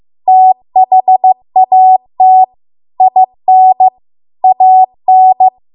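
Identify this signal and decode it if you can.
Morse code, heard as a keyed tone, about 15 wpm, 750 Hz. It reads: THAT IN AN
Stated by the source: Morse code practice at 15 wpm from the national amateur radio association's code archive.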